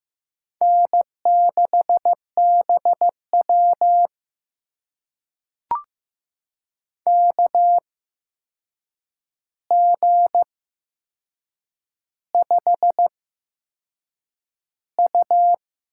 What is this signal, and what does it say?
Morse code, heard as a keyed tone, about 15 wpm, spaced as for 6 wpm, 700 Hz. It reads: N6BW KG5U
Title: USA Call Signs - 4 Characters - 8x Character Spacing - 15wpm